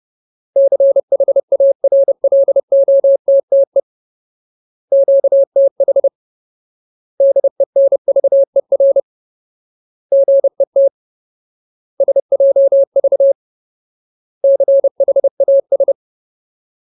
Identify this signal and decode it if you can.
Morse code, heard as a keyed tone, about 30 wpm, 550 Hz. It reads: CHARLOTTE QTH DENVER GET SJV CHAS
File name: Ultimate Word List / 30WPM Ultimate